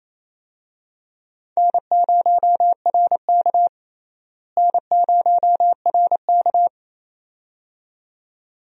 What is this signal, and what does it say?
Morse code, heard as a keyed tone, about 28 wpm, 700 Hz. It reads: N0RK N0RK
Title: USA Call Signs - 4 Characters - Repeated 2x - 28wpm